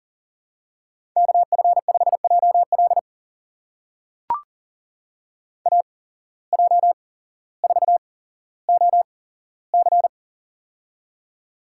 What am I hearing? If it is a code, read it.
Morse code, heard as a keyed tone, about 41 wpm, spaced as for 17 wpm, 700 Hz. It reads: KF5JL AJ4OC
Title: USA Call Signs - 5 Characters - 8x Character Spacing - 40wpm